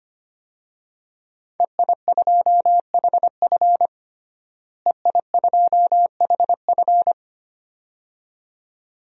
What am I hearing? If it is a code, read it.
Morse code, heard as a keyed tone, about 25 wpm, 700 Hz. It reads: EI2HF EI2HF
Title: DXCC-100 Call Signs - 5 Characters - Repeated 2x - 25wpm